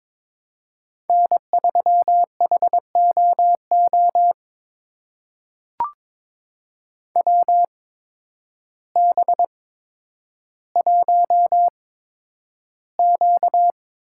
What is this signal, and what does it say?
Morse code, heard as a keyed tone, about 22 wpm, spaced as for 10 wpm, 700 Hz. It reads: N3HOO WB1Q
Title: USA Call Signs - 8x Character Spacing - 22wpm